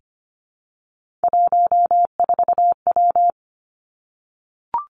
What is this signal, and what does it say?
Morse code, heard as a keyed tone, about 25 wpm, 700 Hz. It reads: 14W
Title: Three Challenging Letter-Number Combinations 25wpm